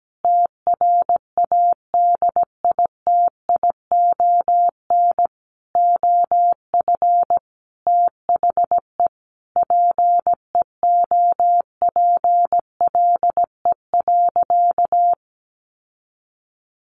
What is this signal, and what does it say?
Morse code, heard as a keyed tone, about 17 wpm, 700 Hz. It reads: TRADITION OF THE PEOPLE.
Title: Sentences from Top 1500 Words 17wpm